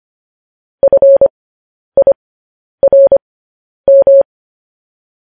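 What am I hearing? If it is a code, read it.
Morse code, heard as a keyed tone, about 25 wpm, spaced as for 10 wpm, 550 Hz. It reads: FIRM
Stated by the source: Morse code news headlines practice at 10 wpm.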